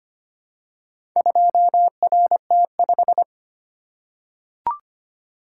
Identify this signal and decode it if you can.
Morse code, heard as a keyed tone, about 25 wpm, 700 Hz. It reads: 2RT5
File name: practice / 25wpm / Four Letter-Number Combinations - 25wpm